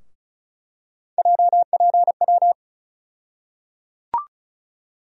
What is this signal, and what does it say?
Morse code, heard as a keyed tone, about 34 wpm, 700 Hz. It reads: JPW
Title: USA Call Sign Postfixes - 3 Letters 35wpm